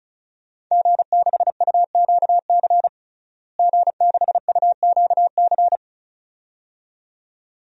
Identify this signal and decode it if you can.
Morse code, heard as a keyed tone, about 36 wpm, 700 Hz. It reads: G6UQC G6UQC
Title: DXCC-100 Call Signs - 5 Characters - Repeated 2x - 35wpm